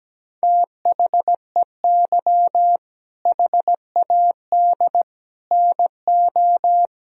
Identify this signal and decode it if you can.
Morse code, heard as a keyed tone, about 17 wpm, 700 Hz. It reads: THEY HAD NO
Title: Sentences from Top 1000 Words - Review Part 1 - 17wpm